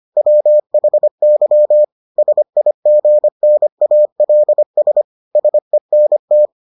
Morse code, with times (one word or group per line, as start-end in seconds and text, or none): 0.2-1.8 WHY
2.2-5.0 SIGNALS
5.4-6.5 SENT